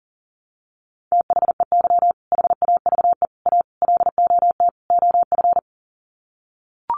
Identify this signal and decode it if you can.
Morse code, heard as a keyed tone, about 40 wpm, 700 Hz. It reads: THEY HAVE A LOT OF
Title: Sets of 5 Words Encore2 40wpm